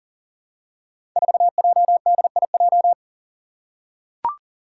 Morse code, heard as a keyed tone, about 42 wpm, 700 Hz. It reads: 4JDIJ
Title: Five Challenging Letter-Number Combinations 40wpm